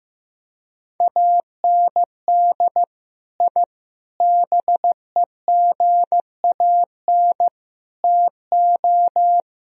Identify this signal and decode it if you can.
Morse code, heard as a keyed tone, about 15 wpm, 700 Hz. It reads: AND I BEGAN TO
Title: Sentences from Top 1000 Words - Review Part 2 - 15wpm